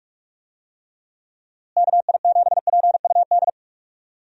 Morse code, heard as a keyed tone, about 46 wpm, 700 Hz. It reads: KI7PUD
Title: USA Call Signs - 6 Characters - Repeated 2x - 45wpm